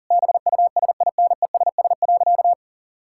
Morse code, heard as a keyed tone, about 40 wpm, 700 Hz. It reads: BUSINESS.
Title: Sentences from Top 1000 Words - Review Part 5 - 40wpm